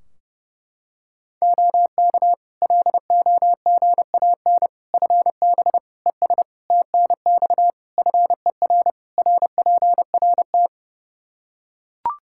Morse code, heard as a keyed tone, about 30 wpm, 700 Hz. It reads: OK LOGAN FB ES TNX FER RPRT
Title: Practice QSO - Exchange 2 - Info - 30wpm